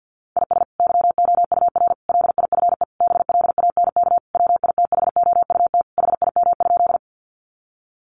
Morse code, heard as a keyed tone, about 50 wpm, 700 Hz. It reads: IS YOUR LIFE BLANK WITHOUT HIM?
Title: Sentences from Top 1500 Words 50wpm